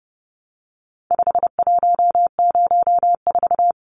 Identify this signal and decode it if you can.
Morse code, heard as a keyed tone, about 30 wpm, 700 Hz. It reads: 5104
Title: Numbers - 4-Digit 30wpm